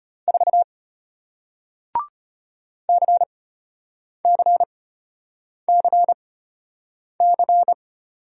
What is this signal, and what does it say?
Morse code, heard as a keyed tone, about 34 wpm, spaced as for 8 wpm, 700 Hz. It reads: 4 CCCC